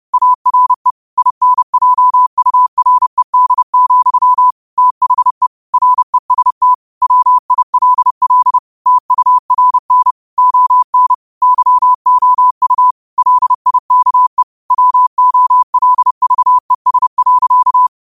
Morse code, heard as a keyed tone, about 30 wpm, 1000 Hz. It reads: ARE INJURED, THE REST WILL TURN ON YOU LIKE WOLVES.
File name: qotdfiles / 30wpm